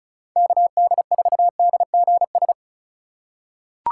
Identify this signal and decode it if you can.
Morse code, heard as a keyed tone, about 36 wpm, 700 Hz. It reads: KD4DGS
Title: USA Call Signs - 6 Characters - 4x Character Spacing - 35wpm